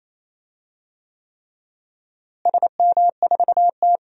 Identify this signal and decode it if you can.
Morse code, heard as a keyed tone, about 28 wpm, 700 Hz. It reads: SM4T
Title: DXCC-100 Call Signs - 4 Characters - Repeated 3x - 28wpm